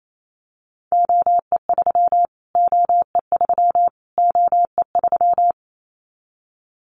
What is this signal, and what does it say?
Morse code, heard as a keyed tone, about 28 wpm, 700 Hz. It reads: OE3 OE3 OE3
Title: Callsign Prefixes for Top 25 DXCC - Repeated 3x - 28wpm